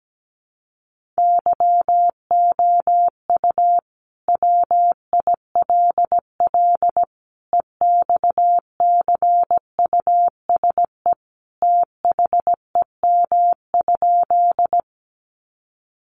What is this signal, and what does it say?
Morse code, heard as a keyed tone, about 17 wpm, 700 Hz. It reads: YOU WILL EXCUSE THEM?
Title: Sentences from Top 700 Words 17wpm